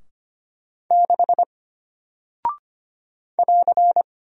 Morse code, heard as a keyed tone, about 25 wpm, 700 Hz. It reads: 6 <AR>
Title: Single Character (Letter:Number:Punctuation:Prosign) Alt 25wpm